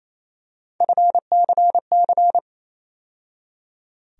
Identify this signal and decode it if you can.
Morse code, heard as a keyed tone, about 28 wpm, 700 Hz. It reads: FCC
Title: USA Call Sign Postfixes - 3 Letters 28wpm